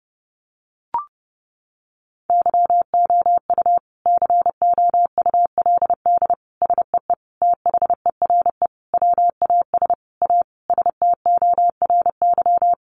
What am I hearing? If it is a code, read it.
Morse code, heard as a keyed tone, about 28 wpm, 700 Hz. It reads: YOU COULD SEE THERE WAS A STORY
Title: Sentences from Top 500 Words 30wpm